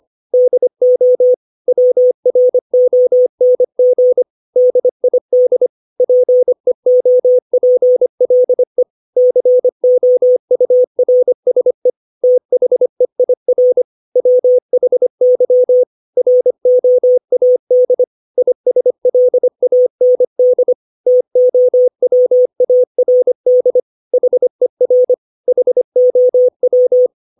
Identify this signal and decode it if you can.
Morse code, heard as a keyed tone, about 25 wpm, 500 Hz. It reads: DO WRONG DID PEOPLE COURSE THEIR WHY ROAD ISLAND TOWARD HER HOW
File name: 632 words / 632 Words 25WPM